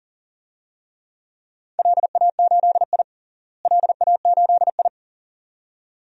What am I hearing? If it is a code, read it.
Morse code, heard as a keyed tone, about 40 wpm, 700 Hz. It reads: LA8I LA8I
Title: DXCC-100 Call Signs - 4 Characters - Repeated 2x - 40wpm